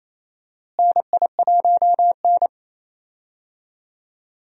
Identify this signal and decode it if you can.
Morse code, heard as a keyed tone, about 28 wpm, 700 Hz. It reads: NI1N
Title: USA Call Signs - 4 Characters - Repeated 3x - 28wpm